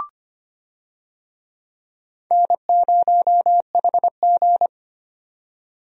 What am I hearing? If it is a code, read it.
Morse code, heard as a keyed tone, about 25 wpm, 700 Hz. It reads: N0HG
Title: USA Call Signs - 4 Characters - 25wpm